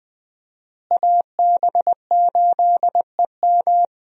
Morse code, heard as a keyed tone, about 20 wpm, 700 Hz. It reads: AB8EM
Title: USA Call Signs - 5 Characters - Repeated 2x - 20wpm